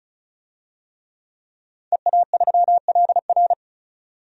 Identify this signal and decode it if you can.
Morse code, heard as a keyed tone, about 34 wpm, 700 Hz. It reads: EA3LR
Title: DXCC-100 Call Signs - 5 Characters - 35wpm